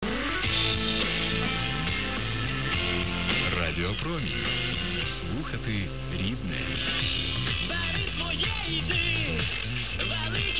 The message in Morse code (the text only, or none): none